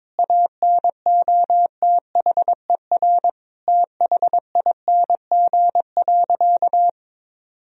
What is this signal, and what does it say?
Morse code, heard as a keyed tone, about 22 wpm, 700 Hz. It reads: ANOTHER THING.
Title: Sentences from Top 1000 Words - Review Part 3 - 22wpm